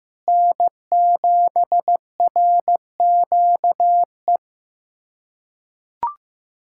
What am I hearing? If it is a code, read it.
Morse code, heard as a keyed tone, about 15 wpm, 700 Hz. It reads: N7RQE
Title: USA Callsign DE Callsign - 5 Characters - 15wpm